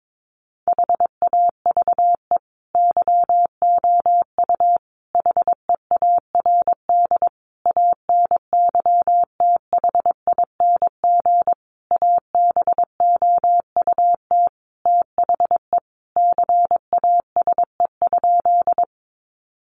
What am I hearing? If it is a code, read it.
Morse code, heard as a keyed tone, about 22 wpm, 700 Hz. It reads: HAVE YOU HEARD ANYTHING ABOUT THE CASE?
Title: Sentences from Top 1000 Words 22wpm